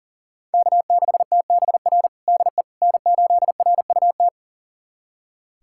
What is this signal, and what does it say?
Morse code, heard as a keyed tone, about 37 wpm, 700 Hz. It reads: K6TBR DE N8RUT
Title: USA Callsign DE Callsign - 5 Characters - 40wpm